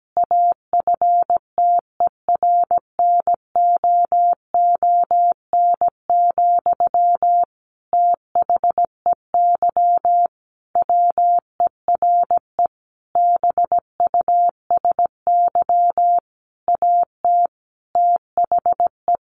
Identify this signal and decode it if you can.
Morse code, heard as a keyed tone, about 17 wpm, 700 Hz. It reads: AFTERNOON, THEY WERE BUSY AT THE
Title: Sentences from Top 1500 Words 17wpm